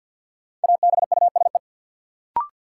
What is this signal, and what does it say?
Morse code, heard as a keyed tone, about 52 wpm, 700 Hz. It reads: ABUSE
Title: Sentences from Top 1300 Words - New Word Drill Alt - 50wpm